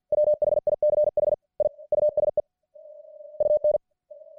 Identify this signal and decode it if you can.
Morse code, heard as a keyed tone, about 50 wpm, 600 Hz. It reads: WHICH I USE UN